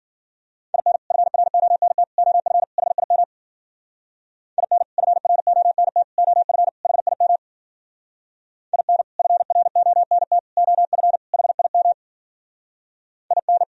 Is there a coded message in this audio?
Yes: Morse code from 0.7 to 13.6 s, heard as a keyed tone, about 50 wpm, 700 Hz. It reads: IN FRONT OF HIM IN FRONT OF HIM IN FRONT OF HIM IN